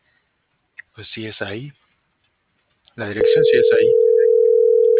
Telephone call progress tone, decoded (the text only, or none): ringback tone